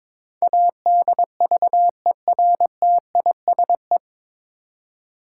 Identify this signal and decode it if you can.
Morse code, heard as a keyed tone, about 21 wpm, 700 Hz. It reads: ADVERTISE
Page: Morse Code Ninja practice 22wpm